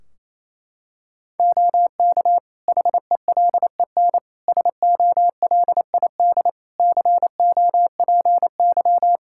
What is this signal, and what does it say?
Morse code, heard as a keyed tone, about 28 wpm, 700 Hz. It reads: OK HELEN SOLID COPY
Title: Practice QSO - Exchange 3 - More Info - 28wpm